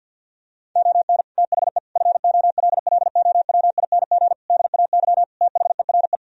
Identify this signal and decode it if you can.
Morse code, heard as a keyed tone, about 50 wpm, 700 Hz. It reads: ON THE FOLLOWING DAY THERE
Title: Sentences from Top 1000 Words - Review Part 4 - 50wpm